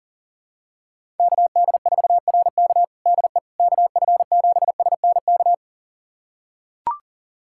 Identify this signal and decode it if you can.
Morse code, heard as a keyed tone, about 37 wpm, 700 Hz. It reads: KD4RK DE KF7SNK
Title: USA Callsign DE Callsign 40wpm